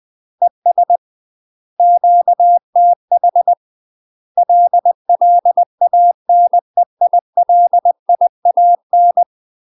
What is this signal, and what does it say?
Morse code, heard as a keyed tone, about 20 wpm, 700 Hz. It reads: ES QTH LLANEILIAN
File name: 20WPM